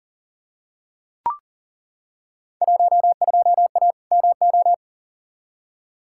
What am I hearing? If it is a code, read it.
Morse code, heard as a keyed tone, about 37 wpm, 700 Hz. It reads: 12A MO
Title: Field Day Exchanges Encore - 40wpm